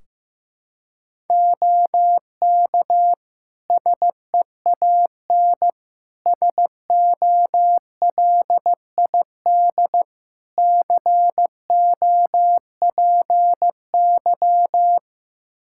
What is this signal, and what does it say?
Morse code, heard as a keyed tone, about 15 wpm, 700 Hz. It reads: OK SEAN SOLID COPY
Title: Practice QSO - Exchange 3 - More Info - 15wpm